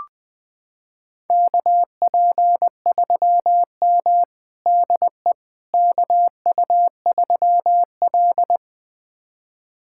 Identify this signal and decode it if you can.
Morse code, heard as a keyed tone, about 20 wpm, 700 Hz. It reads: KP3M DE KU3L
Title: USA Callsign DE Callsign - 4 Characters - 20wpm